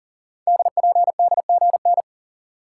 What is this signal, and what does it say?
Morse code, heard as a keyed tone, about 42 wpm, 700 Hz. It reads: DPDGN
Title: Code Groups - 5 Letters - 40wpm